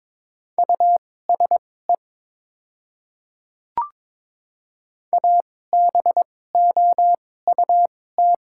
Morse code, heard as a keyed tone, about 22 wpm, spaced as for 16 wpm, 700 Hz. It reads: USE ABOUT
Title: 2x Character Spacing - Top 100 Words - 22wpm